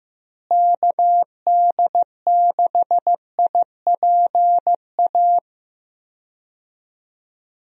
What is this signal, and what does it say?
Morse code, heard as a keyed tone, about 15 wpm, 700 Hz. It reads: KD6IPA
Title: USA Call Signs - 6 Characters - Repeated 2x - 15wpm